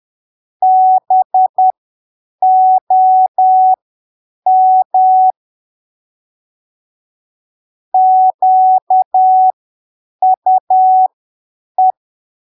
Morse code, heard as a keyed tone, about 10 wpm, spaced as for 7 wpm, 750 Hz. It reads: BOM QUE
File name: arquivos / texto / texto 25 10ppm